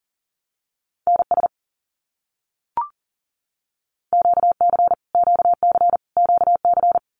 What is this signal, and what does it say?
Morse code, heard as a keyed tone, about 39 wpm, 700 Hz. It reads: NS QC QC QC
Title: Canadian Provinces and Territories - Repeated 3x - 40wpm